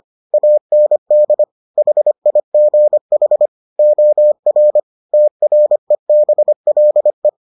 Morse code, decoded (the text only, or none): AND HIGH OR TREBLE